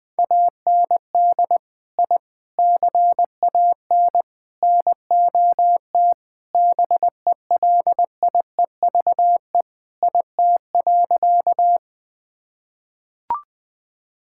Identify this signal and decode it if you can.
Morse code, heard as a keyed tone, about 20 wpm, 700 Hz. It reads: AND I CAN NOT BELIEVE IT.
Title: Sentences with 4-Word Phrases - 20wpm